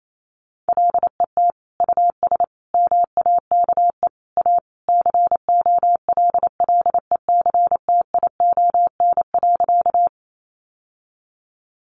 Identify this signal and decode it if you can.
Morse code, heard as a keyed tone, about 28 wpm, 700 Hz. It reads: LET US MAKE A COLLECTION.
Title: Sentences from Top 1200 Words 28wpm